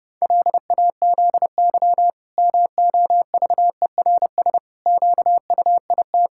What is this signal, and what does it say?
Morse code, heard as a keyed tone, about 32 wpm, 700 Hz. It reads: LAZY MOVERS QUIT
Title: Pangram Challenge v2 - 30wpm